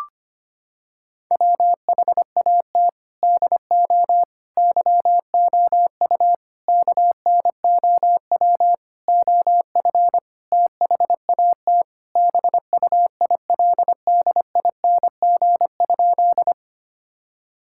Morse code, heard as a keyed tone, about 25 wpm, 700 Hz. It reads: WHAT DO YOU KNOW OF THAT BUILDING?
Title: Sentences from Top 1000 Words - Review Part 5 - 25wpm